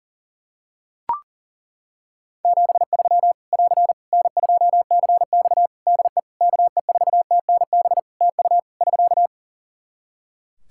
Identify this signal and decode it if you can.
Morse code, heard as a keyed tone, about 40 wpm, 700 Hz. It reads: 73 <AR> N2CX DE KE4TNB TU <SK>
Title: Practice QSO - Exchange 4 - Ending - 40wpm